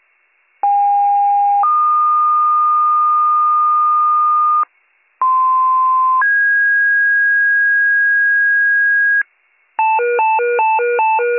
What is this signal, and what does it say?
Two-tone page, heard as a one-second tone, then a three-second tone, about 810 then 1220 Hz, about 1040 then 1650 Hz, with noise. It is heard from 0.6 to 9.2 s.